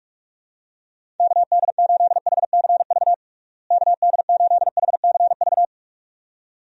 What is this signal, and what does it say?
Morse code, heard as a keyed tone, about 42 wpm, 700 Hz. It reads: KD8HCV KD8HCV
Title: USA Call Signs - 6 Characters - Repeated 2x - 45wpm